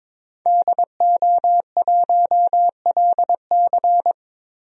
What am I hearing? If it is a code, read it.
Morse code, heard as a keyed tone, about 22 wpm, 700 Hz. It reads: DO1LC
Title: DXCC-100 Call Signs - 5 Characters - Repeated 2x - 22wpm